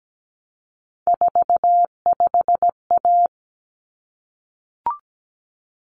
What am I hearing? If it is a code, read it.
Morse code, heard as a keyed tone, about 17 wpm, 700 Hz. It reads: 45A
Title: Common Cut Numbers - 3-Digit 17wpm